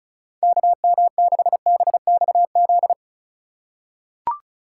Morse code, heard as a keyed tone, about 34 wpm, 700 Hz. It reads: KM6BXZ
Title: USA Call Signs - 6 Characters - 4x Character Spacing - 35wpm